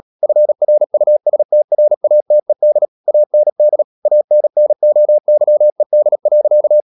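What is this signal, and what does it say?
Morse code, heard as a keyed tone, about 37 wpm, 600 Hz. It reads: FRUSTRATED AND ANNOYED.
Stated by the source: Morse code practice at 35–39 wpm sending English text of Ken Follett's novel Never, from CW med Gnister.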